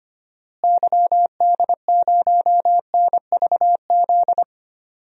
Morse code, heard as a keyed tone, about 25 wpm, 700 Hz. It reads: YD0NVZ
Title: DXCC-100 Call Signs - 6 Characters - Repeated 3x - 25wpm